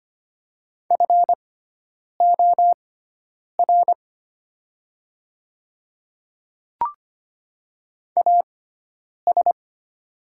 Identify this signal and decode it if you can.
Morse code, heard as a keyed tone, about 25 wpm, spaced as for 9 wpm, 700 Hz. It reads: FOR AS
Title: 6x Character Spacing - Top 100 Words - 25wpm